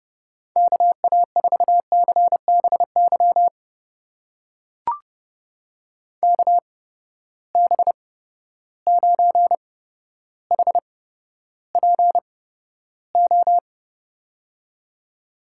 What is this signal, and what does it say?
Morse code, heard as a keyed tone, about 30 wpm, spaced as for 13 wpm, 700 Hz. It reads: KA4CBY KB9HPO
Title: USA Call Signs - 6 Characters - 8x Character Spacing - 30wpm